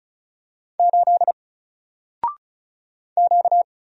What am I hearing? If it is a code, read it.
Morse code, heard as a keyed tone, about 36 wpm, 700 Hz. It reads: 8 Q
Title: Single Letter-Number 35wpm